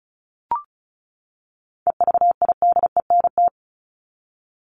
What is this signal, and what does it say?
Morse code, heard as a keyed tone, about 34 wpm, 700 Hz. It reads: EVIDENT
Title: LY Suffix - Top 100 Words with Base Words 35wpm